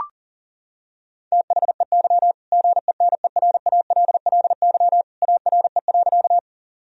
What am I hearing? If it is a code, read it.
Morse code, heard as a keyed tone, about 40 wpm, 700 Hz. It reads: THEY GENERALLY ARE.
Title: Sentences from Top 1000 Words - Review Part 4 - 40wpm